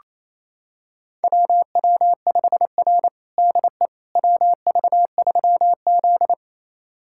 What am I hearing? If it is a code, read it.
Morse code, heard as a keyed tone, about 28 wpm, 700 Hz. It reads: WW5R DE WV3Z
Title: USA Callsign DE Callsign - 4 Characters - 28wpm